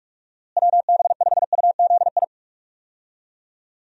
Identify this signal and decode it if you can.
Morse code, heard as a keyed tone, about 47 wpm, 700 Hz. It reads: WB5UZI
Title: USA Call Signs - 6 Characters - Repeated 3x - 45wpm